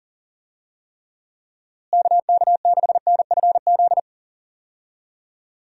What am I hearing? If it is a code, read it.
Morse code, heard as a keyed tone, about 40 wpm, 700 Hz. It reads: KK6NFZ
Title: USA Call Signs - 6 Characters - Repeated 2x - 40wpm